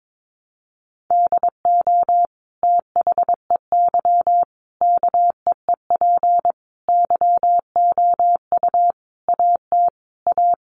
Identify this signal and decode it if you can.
Morse code, heard as a keyed tone, about 22 wpm, 700 Hz. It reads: DO THEY KEEP YOU AT A
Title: Sentences from Top 1200 Words 22wpm